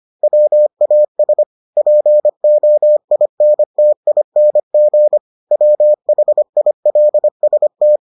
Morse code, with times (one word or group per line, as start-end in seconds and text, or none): 0.2-1.4 WAS
1.8-5.2 POINTING
5.5-8.0 WHILST